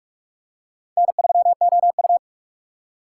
Morse code, heard as a keyed tone, about 46 wpm, 700 Hz. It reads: N3OU
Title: USA Call Signs - 4 Characters - 3x Character Spacing - 45wpm